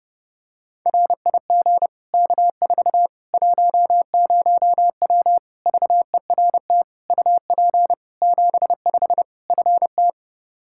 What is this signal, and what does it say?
Morse code, heard as a keyed tone, about 30 wpm, 700 Hz. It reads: RIG K4 10W VERT UP 75 FT
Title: Practice QSO - Exchange 2 - Info - 30wpm